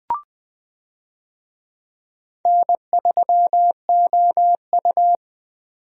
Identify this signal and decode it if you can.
Morse code, heard as a keyed tone, about 20 wpm, 700 Hz. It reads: N3OU
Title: USA Call Signs - 4 Characters - 20wpm